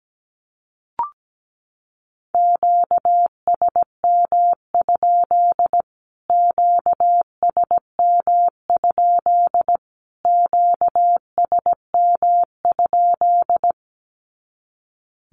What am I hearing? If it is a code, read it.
Morse code, heard as a keyed tone, about 17 wpm, 700 Hz. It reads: QSM? QSM? QSM?
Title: Q-Codes Repeated 3x 17wpm